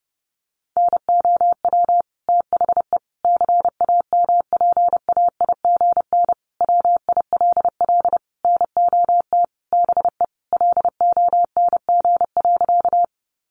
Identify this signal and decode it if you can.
Morse code, heard as a keyed tone, about 30 wpm, 700 Hz. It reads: NOW THE CAMPAIGN WILL NOT BE LONG.
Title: Sentences from Top 1200 Words 30wpm